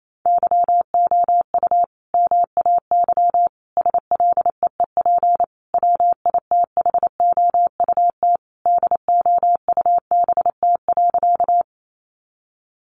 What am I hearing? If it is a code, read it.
Morse code, heard as a keyed tone, about 28 wpm, 700 Hz. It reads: YOU MAY SLEEP WITHOUT DOUBT.
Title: Sentences from Top 1500 Words 28wpm